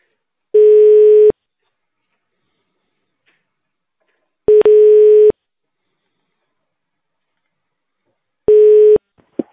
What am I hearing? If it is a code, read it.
Ringback tone (425 Hz, 1 s on, 3 s off).